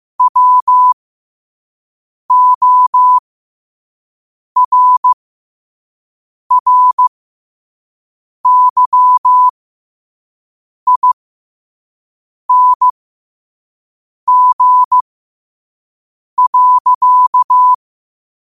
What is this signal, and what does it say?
Morse code, heard as a keyed tone, about 15 wpm, spaced as for 5 wpm, 1000 Hz. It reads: WORRYING.
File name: qotdfiles / 5wpm